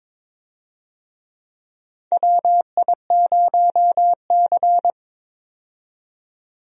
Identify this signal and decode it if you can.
Morse code, heard as a keyed tone, about 21 wpm, 700 Hz. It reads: WI0C
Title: USA Call Signs - 4 Characters - 22wpm